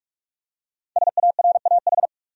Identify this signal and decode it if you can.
Morse code, heard as a keyed tone, about 46 wpm, 700 Hz. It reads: SARAH
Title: USA Common Names - Max 5 Letters - 45wpm